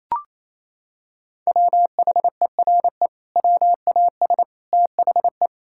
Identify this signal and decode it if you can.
Morse code, heard as a keyed tone, about 27 wpm, 700 Hz. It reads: WHERE WAS THE
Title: Sentences from Top 1600 Words 28wpm